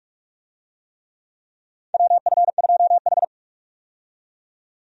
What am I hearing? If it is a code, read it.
Morse code, heard as a keyed tone, about 46 wpm, 700 Hz. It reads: WF2H
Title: USA Call Signs - 4 Characters - 45wpm